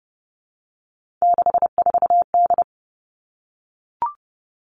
Morse code, heard as a keyed tone, about 30 wpm, 700 Hz. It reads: 64D
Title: Three Challenging Letter-Number Combinations 30wpm